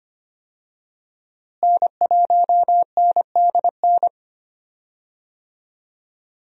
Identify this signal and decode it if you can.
Morse code, heard as a keyed tone, about 25 wpm, 700 Hz. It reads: N1NDN
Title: USA Call Signs - 5 Characters - Repeated 2x - 25wpm